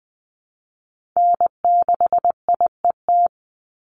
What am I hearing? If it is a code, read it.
Morse code, heard as a keyed tone, about 20 wpm, 700 Hz. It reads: N6IET
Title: Call Signs Repeated 2x - 20wpm